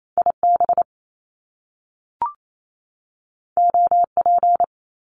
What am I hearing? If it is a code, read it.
Morse code, heard as a keyed tone, about 28 wpm, 700 Hz. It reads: IB OP
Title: Top Two Letter Combinations in Words 28wpm